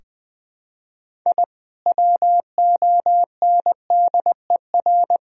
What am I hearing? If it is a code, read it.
Morse code, heard as a keyed tone, about 21 wpm, 700 Hz. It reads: I WONDER
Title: Sentences from Top 1000 Words - Review Part 3 - 20wpm